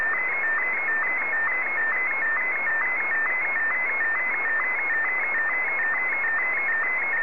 SSTV (slow-scan television), Martin 1.